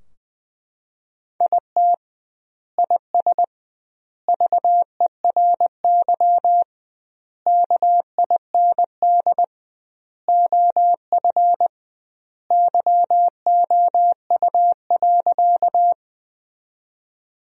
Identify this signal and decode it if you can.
Morse code, heard as a keyed tone, about 20 wpm, 700 Hz. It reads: IT IS VERY KIND OF YOU.
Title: 2x Word-Spacing - Sentences from Top 200 Words - 20wpm